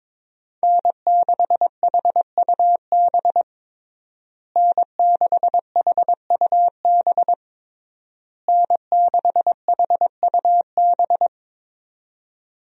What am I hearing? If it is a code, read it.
Morse code, heard as a keyed tone, about 22 wpm, 700 Hz. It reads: N6HUB N6HUB N6HUB
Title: USA Call Signs - 5 Characters - Repeated 3x - 22wpm